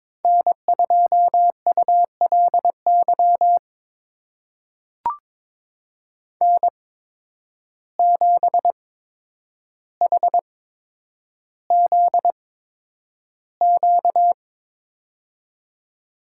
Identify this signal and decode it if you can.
Morse code, heard as a keyed tone, about 22 wpm, spaced as for 9 wpm, 700 Hz. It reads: N2ULY N7HZQ
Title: USA Call Signs - 5 Characters - 8x Character Spacing - 22wpm